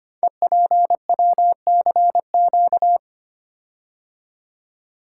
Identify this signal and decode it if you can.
Morse code, heard as a keyed tone, about 25 wpm, 700 Hz. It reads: EPWCQ
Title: Code Groups - 5 Letter:Numbers - 25wpm